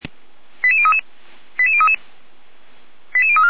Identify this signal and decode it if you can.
Five-tone selective call (ZVEI), heard as quick runs of five tones, tones of about 70 ms. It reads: 88033 88033 88034